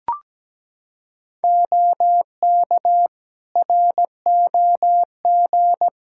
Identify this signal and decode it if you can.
Morse code, heard as a keyed tone, about 17 wpm, 700 Hz. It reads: OK ROG
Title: Practice QSO - Exchange 3 - More Info - 17wpm